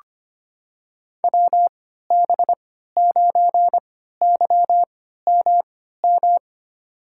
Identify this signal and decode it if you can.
Morse code, heard as a keyed tone, about 25 wpm, spaced as for 14 wpm, 700 Hz. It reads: WB9YMM
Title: USA Call Signs - 6 Characters - 3x Character Spacing - 25wpm